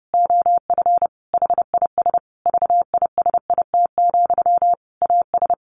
Morse code, heard as a keyed tone, about 30 wpm, 700 Hz. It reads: OF HIS VISIT, AS
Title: Sentences from Top 1400 Words 30wpm